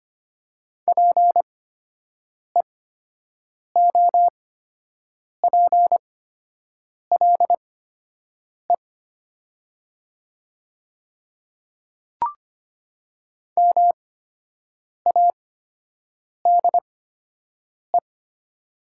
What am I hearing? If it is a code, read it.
Morse code, heard as a keyed tone, about 25 wpm, spaced as for 7 wpm, 700 Hz. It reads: PEOPLE MADE
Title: 8x Character Spacing - Top 200 Words - 25wpm